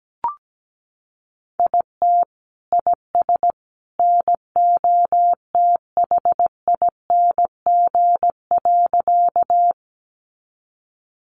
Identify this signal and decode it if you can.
Morse code, heard as a keyed tone, about 17 wpm, 700 Hz. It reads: IT IS NOTHING.
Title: Sentences from Top 500 Words 17wpm